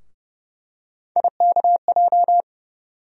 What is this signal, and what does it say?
Morse code, heard as a keyed tone, about 30 wpm, 700 Hz. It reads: IKJ
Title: USA Call Sign Postfixes - 3 Letters 30wpm